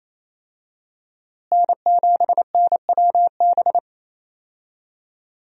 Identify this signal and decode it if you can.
Morse code, heard as a keyed tone, about 27 wpm, 700 Hz. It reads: N7NWB
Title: USA Call Signs - 5 Characters - Repeated 2x - 28wpm